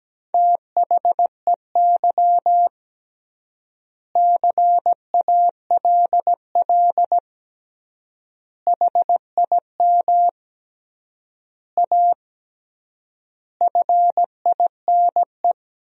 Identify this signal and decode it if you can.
Morse code, heard as a keyed tone, about 17 wpm, 700 Hz. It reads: THEY CALL HIM A FINE
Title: 3x Word-Spacing - Sentences from Top 500 Words - 17wpm